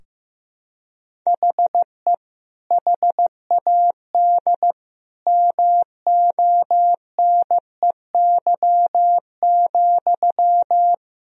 Morse code, heard as a keyed tone, about 15 wpm, 700 Hz. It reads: HE HAD MONEY,
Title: Sentences from Top 1000 Words - Review Part 3 - 15wpm